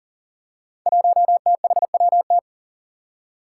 Morse code, heard as a keyed tone, about 40 wpm, 700 Hz. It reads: 1THWT
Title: Five Challenging Letter-Number Combinations 40wpm